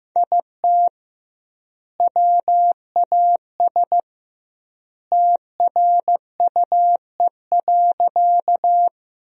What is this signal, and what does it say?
Morse code, heard as a keyed tone, about 15 wpm, 700 Hz. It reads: IT WAS TRUE.
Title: 2x Word-Spacing - Sentences from Top 400 Words - 15wpm